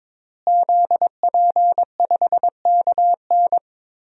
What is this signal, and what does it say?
Morse code, heard as a keyed tone, about 21 wpm, 700 Hz. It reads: ZP5KN
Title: DXCC-100 Call Signs - 5 Characters - Repeated 2x - 22wpm